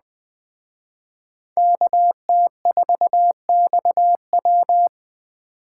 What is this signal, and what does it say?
Morse code, heard as a keyed tone, about 20 wpm, 700 Hz. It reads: KT4XW